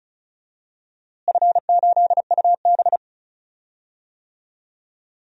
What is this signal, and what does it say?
Morse code, heard as a keyed tone, about 34 wpm, 700 Hz. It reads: F8UB
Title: DXCC-100 Call Signs - 4 Characters - Repeated 2x - 35wpm